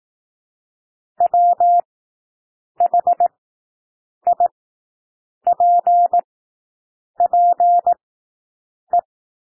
Morse code, heard as a keyed tone, about 18 wpm, spaced as for 7 wpm, 700 Hz. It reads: WHIPPE
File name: SOUND WORDS 18 wpm